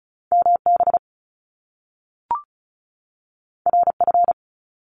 Morse code, heard as a keyed tone, about 36 wpm, 700 Hz. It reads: MB RF
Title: Top Two Letter Combinations in Words 35wpm